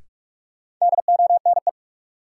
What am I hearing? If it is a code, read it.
Morse code, heard as a keyed tone, about 46 wpm, 700 Hz. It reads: DONE
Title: Sets of 5 Words Encore - New Word Drill - 45wpm